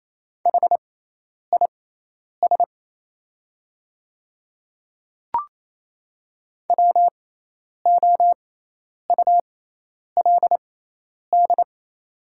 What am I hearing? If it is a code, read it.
Morse code, heard as a keyed tone, about 28 wpm, spaced as for 10 wpm, 700 Hz. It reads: HIS WOULD
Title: 6x Character Spacing - Top 200 Words - 28wpm